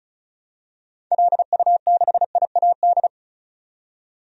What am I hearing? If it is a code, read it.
Morse code, heard as a keyed tone, about 36 wpm, 700 Hz. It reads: LU6IAD
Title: DXCC-100 Call Signs - 6 Characters - 35wpm